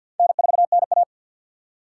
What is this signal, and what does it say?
Morse code, heard as a keyed tone, about 46 wpm, 700 Hz. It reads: N4NA
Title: USA Call Signs - 4 Characters - 2x Character Spacing - 50wpm